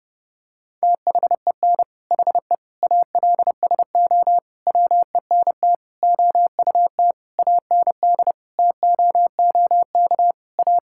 Morse code, heard as a keyed tone, about 32 wpm, 700 Hz. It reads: THEN HE ALSO WENT OUT AND TOOK A
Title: Sentences from Top 1600 Words 30wpm